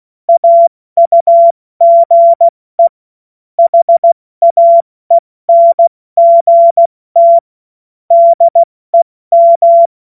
Morse code, heard as a keyed tone, about 16 wpm, 680 Hz. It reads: AUGE HAENGT DEM
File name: Morsekurs / Witze / Witz 19 Dezember 2023 70 F80